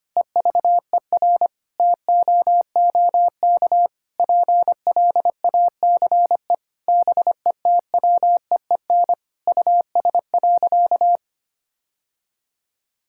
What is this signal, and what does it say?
Morse code, heard as a keyed tone, about 25 wpm, 700 Hz. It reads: EVER TOOK PLACE BETWEEN US.
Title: Sentences from Top 1000 Words - Review Part 6 - 25wpm